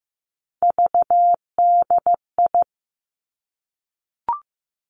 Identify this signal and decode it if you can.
Morse code, heard as a keyed tone, about 15 wpm, 700 Hz. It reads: VDI
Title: Three Challenging Letter-Number Combinations 15wpm